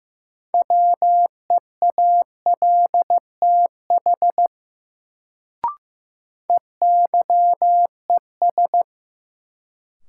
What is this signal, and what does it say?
Morse code, heard as a keyed tone, about 15 wpm, 700 Hz. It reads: WEALTH EYES
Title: Sentences from Top 1000 Words - Review Part 4 - New Word Drill - 15wpm